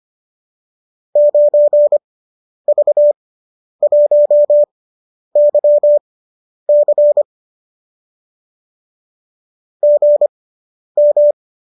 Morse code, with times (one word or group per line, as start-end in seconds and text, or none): 1.2-7.2 9V1YC
9.8-11.3 GM